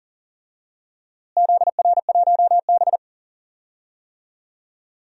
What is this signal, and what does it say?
Morse code, heard as a keyed tone, about 40 wpm, 700 Hz. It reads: ZR1B
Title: DXCC-100 Call Signs - 4 Characters - Repeated 2x - 40wpm